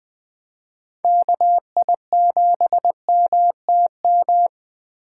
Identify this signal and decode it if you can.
Morse code, heard as a keyed tone, about 24 wpm, 700 Hz. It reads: KI7MTM
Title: Speed Racing - USA 2x3 Call Signs 20wpm